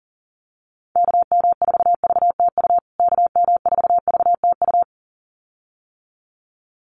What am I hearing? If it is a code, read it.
Morse code, heard as a keyed tone, about 37 wpm, 700 Hz. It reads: KM4VTU KM4VTU
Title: USA Call Signs Repeated 2x 40wpm